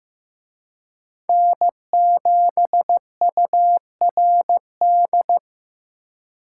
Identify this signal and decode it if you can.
Morse code, heard as a keyed tone, about 18 wpm, 700 Hz. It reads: N7URD
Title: Speed Racing - USA 1x3 Call Signs 15wpm